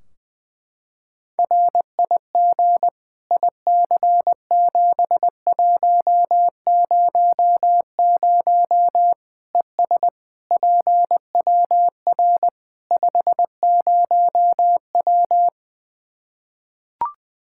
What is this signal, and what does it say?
Morse code, heard as a keyed tone, about 20 wpm, 700 Hz. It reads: RIG IC7100 ES PWR 50W
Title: Practice QSO - Exchange 2 - Info - 20wpm